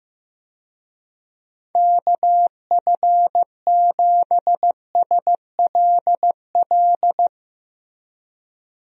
Morse code, heard as a keyed tone, about 15 wpm, 700 Hz. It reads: KF7SLL
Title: USA Call Signs - 6 Characters - Repeated 2x - 15wpm